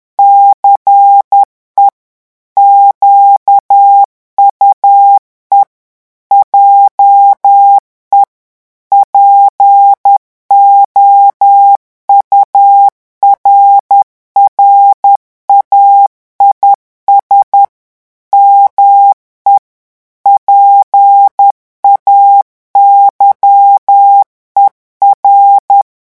Morse code, heard as a keyed tone, about 10 wpm, 800 Hz. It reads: CE QUE JE POURRAIS ME PAYER
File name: revision9 citations humoristiques vitesse 10 mots minute